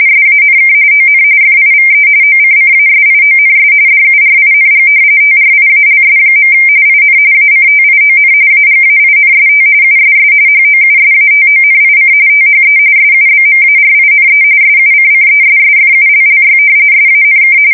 RTTY, about 45.45 baud. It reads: SANK ON FIVE OCCASIONS DURING TRIALS. IN ACTION AGAINST USS HOUSATONIC THE SUBMARINE SUCCESSFULLY SANK
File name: fsk170 / usnavy3